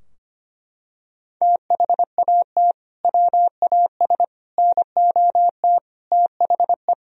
Morse code, heard as a keyed tone, about 25 wpm, 700 Hz. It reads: THAT WAS NOT THE
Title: Sentences from Top 1000 Words - Review Part 2 - 25wpm